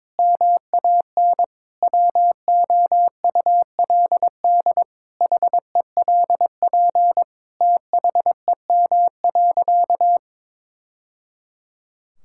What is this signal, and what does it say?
Morse code, heard as a keyed tone, about 22 wpm, 700 Hz. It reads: MAN WOULD HELP THEM.
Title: Sentences from Top 1000 Words - Review Part 5 - 22wpm